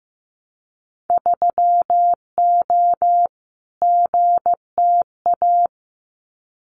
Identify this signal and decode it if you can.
Morse code, heard as a keyed tone, about 15 wpm, 700 Hz. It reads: 3O GTA
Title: Winter Field Day Exchanges Encore 15wpm